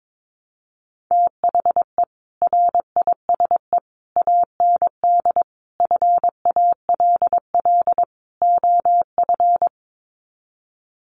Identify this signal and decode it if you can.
Morse code, heard as a keyed tone, about 22 wpm, 700 Hz. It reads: THE RISE AND FALL OF
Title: Sets of 5 Words Encore2 22wpm